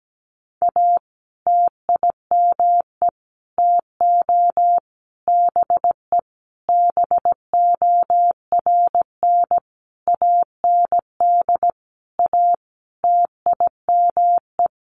Morse code, heard as a keyed tone, about 17 wpm, 700 Hz. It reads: A TIME TO BE BORN AND A TIME
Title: Sentences from Top 1300 Words 17wpm